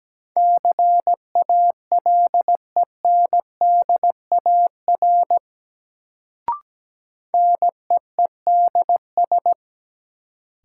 Morse code, heard as a keyed tone, about 17 wpm, 700 Hz. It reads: CALENDAR NEEDS
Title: Sentences from Top 1600 Words - New Word Drill Alt - 17wpm